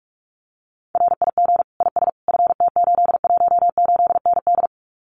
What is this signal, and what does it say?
Morse code, heard as a keyed tone, about 46 wpm, 700 Hz. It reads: RIG IS FT818ND
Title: Common QSO Phrases Encore 45wpm